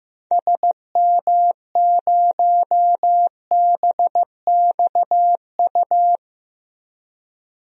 Morse code, heard as a keyed tone, about 15 wpm, 700 Hz. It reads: SM0BXU
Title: DXCC-100 Call Signs - 6 Characters - 15wpm